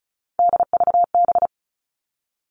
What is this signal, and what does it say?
Morse code, heard as a keyed tone, about 34 wpm, 700 Hz. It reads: DVB
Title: All Cut Numbers - 3-Digit 35wpm